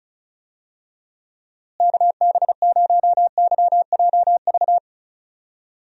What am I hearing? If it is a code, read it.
Morse code, heard as a keyed tone, about 36 wpm, 700 Hz. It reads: KB0YJV
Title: USA Call Signs - 6 Characters - Repeated 2x - 35wpm